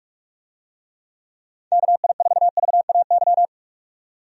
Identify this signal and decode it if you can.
Morse code, heard as a keyed tone, about 46 wpm, 700 Hz. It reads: KI4VAY
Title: USA Call Signs - 6 Characters - 45wpm